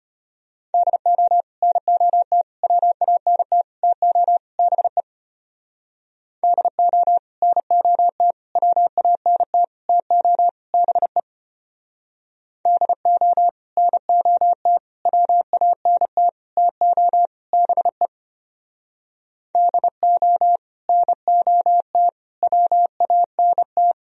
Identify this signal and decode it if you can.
Morse code, heard as a keyed tone, about 33 wpm, 700 Hz. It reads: DO NOT WANT TO BE DO NOT WANT TO BE DO NOT WANT TO BE DO NOT WANT